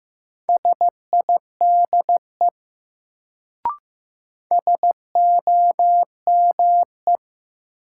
Morse code, heard as a keyed tone, about 15 wpm, 700 Hz. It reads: SIDE SOME
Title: Sets of 2 Words - New Word Drill - 15wpm